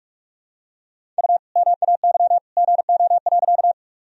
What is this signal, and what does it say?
Morse code, heard as a keyed tone, about 45 wpm, 700 Hz. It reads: U MAY GO.
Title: Sentences from Top 100 Words and Abbreviations 45wpm